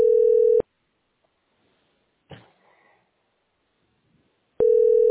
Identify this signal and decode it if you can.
Ringback tone (North America), 440 + 480 Hz.